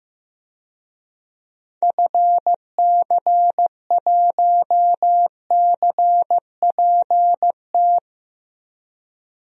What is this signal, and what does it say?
Morse code, heard as a keyed tone, about 15 wpm, 700 Hz. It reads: FC1CPT
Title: DXCC-100 Call Signs - 6 Characters - 15wpm